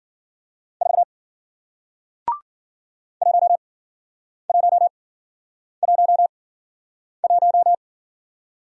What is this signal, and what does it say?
Morse code, heard as a keyed tone, about 53 wpm, spaced as for 9 wpm, 700 Hz. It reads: 4 1111